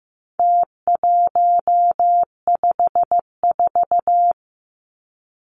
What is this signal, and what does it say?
Morse code, heard as a keyed tone, about 15 wpm, 700 Hz. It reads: T154
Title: Numbers - Contest Style - 4-Digit 15wpm